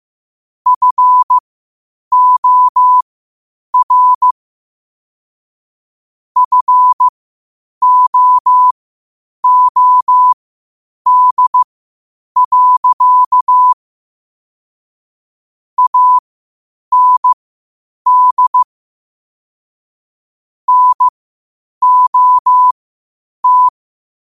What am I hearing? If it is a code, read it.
Morse code, heard as a keyed tone, about 15 wpm, spaced as for 8 wpm, 1000 Hz. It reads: FOR FOOD. AND NOT
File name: qotdfiles / 8wpm